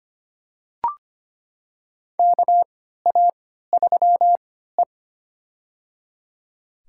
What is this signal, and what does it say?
Morse code, heard as a keyed tone, about 25 wpm, spaced as for 14 wpm, 700 Hz. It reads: KA3E
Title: USA Call Signs - 4 Characters - 3x Character Spacing - 25wpm